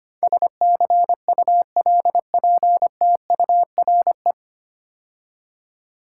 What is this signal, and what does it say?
Morse code, heard as a keyed tone, about 25 wpm, 700 Hz. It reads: SCULPTURE
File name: practice / 25wpm / 9 Letter Words - First 3 Letters Given - 25wpm